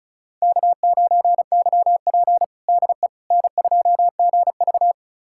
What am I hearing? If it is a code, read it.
Morse code, heard as a keyed tone, about 34 wpm, 700 Hz. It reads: K9YP DE N2GV
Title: USA Callsign DE Callsign - 4 Characters - 35wpm